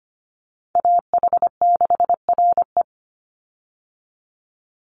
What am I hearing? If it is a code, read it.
Morse code, heard as a keyed tone, about 25 wpm, 700 Hz. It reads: AH6RE
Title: Call Signs Repeated 3x - 25wpm